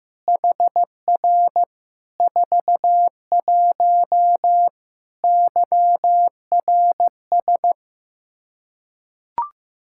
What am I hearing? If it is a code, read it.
Morse code, heard as a keyed tone, about 15 wpm, 700 Hz. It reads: HR 41 YRS
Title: Practice QSO - Exchange 3 - More Info - 15wpm